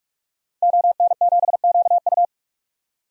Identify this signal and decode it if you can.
Morse code, heard as a keyed tone, about 46 wpm, 700 Hz. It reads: ON7QU
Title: DXCC-100 Call Signs - 5 Characters - 45wpm